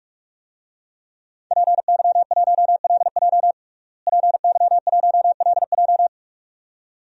Morse code, heard as a keyed tone, about 46 wpm, 700 Hz. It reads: PY1LJ PY1LJ
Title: DXCC-100 Call Signs - 5 Characters - Repeated 2x - 45wpm